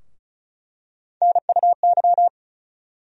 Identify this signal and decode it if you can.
Morse code, heard as a keyed tone, about 36 wpm, 700 Hz. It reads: NUY
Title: USA Call Sign Postfixes - 3 Letters 35wpm